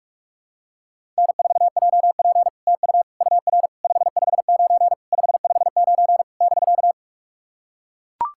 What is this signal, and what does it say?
Morse code, heard as a keyed tone, about 44 wpm, 690 Hz. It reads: N4JP TU UR 559 559 <BK>
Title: SOTA QSO Practice 45wpm